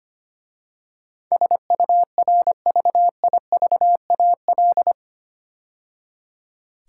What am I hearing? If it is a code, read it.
Morse code, heard as a keyed tone, about 25 wpm, 700 Hz. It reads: SURVIVAL
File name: practice / 25wpm / 8 Letter Words - First 3 Letters Given - 25wpm